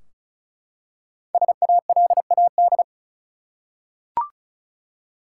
Morse code, heard as a keyed tone, about 36 wpm, 700 Hz. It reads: SALAD
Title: Five Letter Words - 35wpm